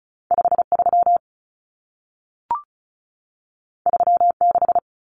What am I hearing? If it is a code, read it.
Morse code, heard as a keyed tone, about 36 wpm, 700 Hz. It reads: 53 36
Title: Common Cut Numbers - 2-Digit 35wpm